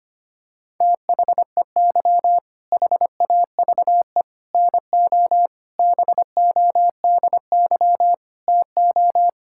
Morse code, heard as a keyed tone, about 25 wpm, 700 Hz. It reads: THEY HAVE NO BODY TO
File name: practice / 25wpm / All-Age Appropriate Jokes - 25wpm